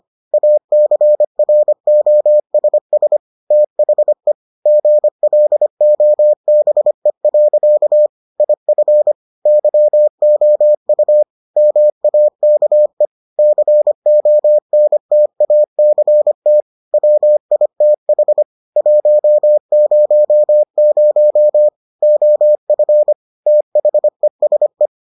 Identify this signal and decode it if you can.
Morse code, heard as a keyed tone, about 25 wpm, 600 Hz. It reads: ACROSS THE GLOBE. IF YOU MAKE CONTACT WITH 100 OF THESE